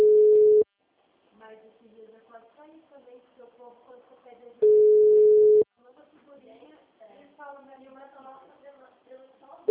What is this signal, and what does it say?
Ringback tone (Europe, 425 Hz), with speech.